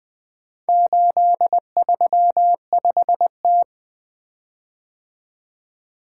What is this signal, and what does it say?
Morse code, heard as a keyed tone, about 20 wpm, 700 Hz. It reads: 835T
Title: Four Letter-Number Combinations - 20wpm